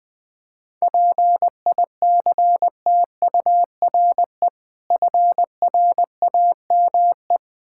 Morse code, heard as a keyed tone, about 20 wpm, 700 Hz. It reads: PICTURE FRAME
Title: Difficult Words with Hints - 20wpm